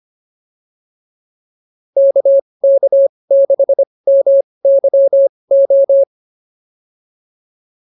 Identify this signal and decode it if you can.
Morse code, heard as a keyed tone, about 25 wpm, spaced as for 20 wpm, 550 Hz. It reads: KK6MYO